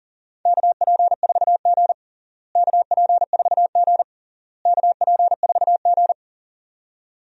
Morse code, heard as a keyed tone, about 40 wpm, 700 Hz. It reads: KP4G KP4G KP4G
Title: DXCC-100 Call Signs - 4 Characters - Repeated 3x - 40wpm